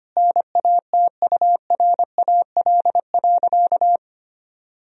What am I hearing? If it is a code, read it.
Morse code, heard as a keyed tone, about 25 wpm, 700 Hz. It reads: NATURAL.
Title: Sentences from Top 1000 Words - Review Part 4 - 25wpm